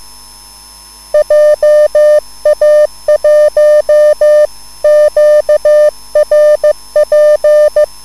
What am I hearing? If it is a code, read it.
Morse code, heard as a keyed tone, about 15 wpm, 600 Hz. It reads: JA1QRP